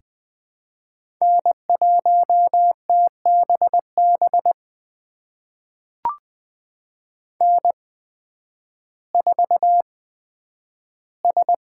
Morse code, heard as a keyed tone, about 21 wpm, spaced as for 11 wpm, 700 Hz. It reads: N1TBB N4S
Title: USA Call Signs - 5 Characters - 8x Character Spacing - 20wpm